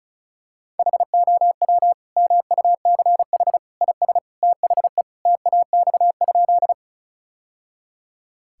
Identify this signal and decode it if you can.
Morse code, heard as a keyed tone, about 36 wpm, 700 Hz. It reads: HOW MUCH IS THE TAX?
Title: Sentences from Top 1000 Words - Review Part 5 - 35wpm